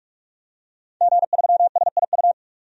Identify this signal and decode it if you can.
Morse code, heard as a keyed tone, about 44 wpm, 690 Hz. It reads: G3SIU
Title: DXCC-100 Call Signs - 5 Characters - 45wpm